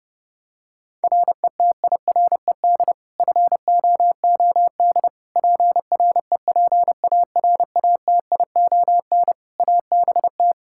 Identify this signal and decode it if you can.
Morse code, heard as a keyed tone, about 30 wpm, 700 Hz. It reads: RETIRED FOOD PREPARATION ABT
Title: K1USN SST QSO Practice 30wpm